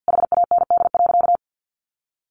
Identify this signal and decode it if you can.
Morse code, heard as a keyed tone, about 50 wpm, 700 Hz. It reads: HAND.
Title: Sentences from Top 1400 Words 50wpm